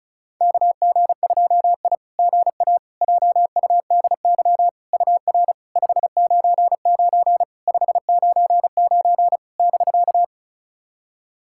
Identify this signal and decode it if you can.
Morse code, heard as a keyed tone, about 34 wpm, 690 Hz. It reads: KG2I GA JUDY UR 599 599 <BK>
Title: SOTA QSO Practice 35wpm